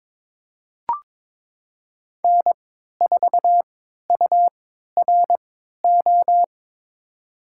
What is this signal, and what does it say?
Morse code, heard as a keyed tone, about 21 wpm, spaced as for 12 wpm, 700 Hz. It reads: N4URO
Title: USA Call Signs - 5 Characters - 3x Character Spacing - 22wpm